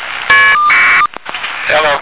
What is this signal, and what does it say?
Caller ID data burst, with speech.